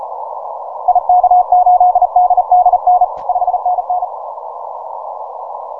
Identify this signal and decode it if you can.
Morse code, heard as a keyed tone, about 36 wpm, 720 Hz. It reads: IK8DDN 5NN
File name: simple-file-list / VK0EK 10MCW